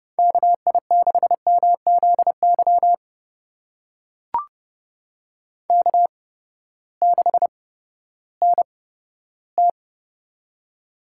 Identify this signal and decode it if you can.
Morse code, heard as a keyed tone, about 30 wpm, spaced as for 15 wpm, 700 Hz. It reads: KI6MZY K6NT
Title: USA Call Signs - 8x Character Spacing - 30wpm